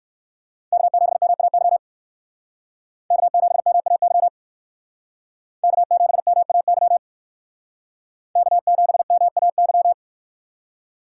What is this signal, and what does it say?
Morse code, heard as a keyed tone, about 59 wpm, 700 Hz. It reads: K7MAY K7MAY K7MAY K7MAY